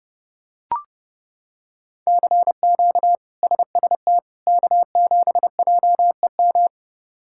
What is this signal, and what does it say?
Morse code, heard as a keyed tone, about 30 wpm, 700 Hz. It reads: CQ SST K7JEM
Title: K1USN SST QSO Practice - 30wpm